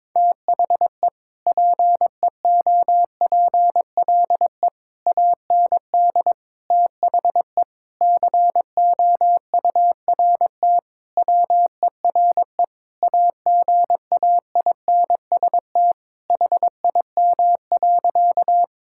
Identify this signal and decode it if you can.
Morse code, heard as a keyed tone, about 22 wpm, 700 Hz. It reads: THE PEOPLE AND THE COURT WERE AGAINST HIM.
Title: Sentences from Top 1600 Words 22wpm